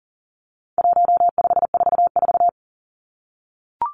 Morse code, heard as a keyed tone, about 40 wpm, 700 Hz. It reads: 1544
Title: Numbers - 4-Digit 40wpm